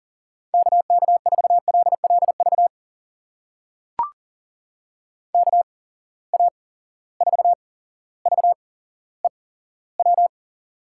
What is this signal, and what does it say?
Morse code, heard as a keyed tone, about 37 wpm, spaced as for 17 wpm, 700 Hz. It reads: KK4LLV KA4VEW
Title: USA Call Signs - 6 Characters - 8x Character Spacing - 40wpm